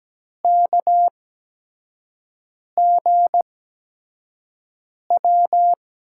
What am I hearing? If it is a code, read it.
Morse code, heard as a keyed tone, about 17 wpm, spaced as for 5 wpm, 700 Hz. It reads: KGW
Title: USA Call Signs - 8x Character Spacing - 17wpm